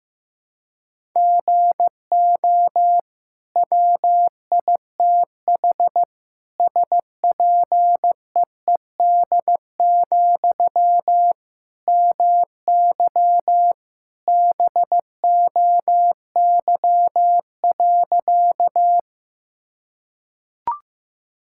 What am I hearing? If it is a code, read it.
Morse code, heard as a keyed tone, about 16 wpm, 700 Hz. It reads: GO WITH SPEED, MY BOY.
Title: Sentences from Top 1000 Words - Review Part 5 - 15wpm